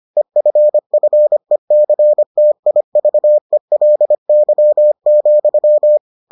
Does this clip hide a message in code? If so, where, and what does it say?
Yes: Morse code from 0.2 to 6.0 s, heard as a keyed tone, about 25 wpm, 600 Hz. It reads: EFFECTIVELY,